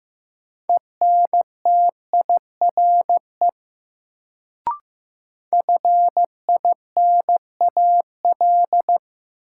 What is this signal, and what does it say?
Morse code, heard as a keyed tone, about 15 wpm, 700 Hz. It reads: ENTIRE FINAL
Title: Sentences from Top 800 Words - New Word Drill - 15wpm